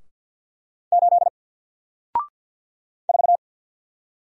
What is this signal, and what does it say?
Morse code, heard as a keyed tone, about 48 wpm, 700 Hz. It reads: 8 4
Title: Single Character (Letter:Number:Punctuation:Prosign) Alt 50wpm